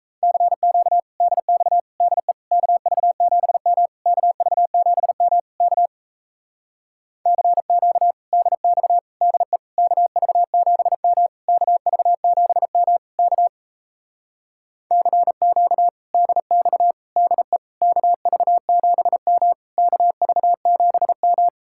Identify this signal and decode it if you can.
Morse code, heard as a keyed tone, about 37 wpm, 700 Hz. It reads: CQ DX DE KV7M KV7M K CQ DX DE KV7M KV7M K CQ DX DE KV7M KV7M